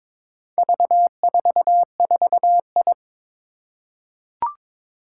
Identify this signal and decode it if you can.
Morse code, heard as a keyed tone, about 22 wpm, 700 Hz. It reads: V44I
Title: Four Challenging Letter-Number Combinations 22wpm